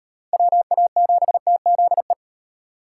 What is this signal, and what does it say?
Morse code, heard as a keyed tone, about 38 wpm, 700 Hz. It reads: WA7TZE